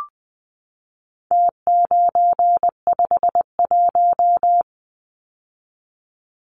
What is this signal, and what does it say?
Morse code, heard as a keyed tone, about 20 wpm, 700 Hz. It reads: T951
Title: Numbers - Contest Style - 4-Digit 20wpm